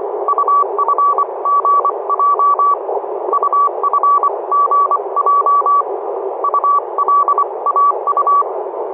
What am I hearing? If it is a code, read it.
Morse code, heard as a keyed tone, about 25 wpm, spaced as for 22 wpm, 1140 Hz, with noise. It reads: UFGJ UFGJ ULAU
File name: oxb-tfc-list